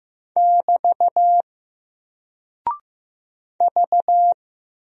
Morse code, heard as a keyed tone, about 15 wpm, 700 Hz. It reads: <BT> V
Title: Single Character (Letter:Number:Punctuation:Prosign) 15wpm